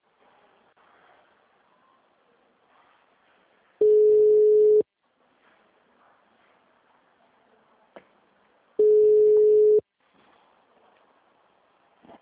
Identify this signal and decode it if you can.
Ringback tone (Europe, 425 Hz).